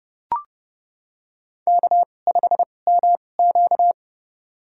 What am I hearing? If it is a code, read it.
Morse code, heard as a keyed tone, about 30 wpm, spaced as for 22 wpm, 700 Hz. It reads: K5MQ
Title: USA Call Signs - 2x Character Spacing - 30wpm